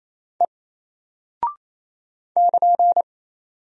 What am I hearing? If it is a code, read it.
Morse code, heard as a keyed tone, about 28 wpm, 700 Hz. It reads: E <KN>
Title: Single Character (Letter:Number:Punctuation:Prosign) Alt 28wpm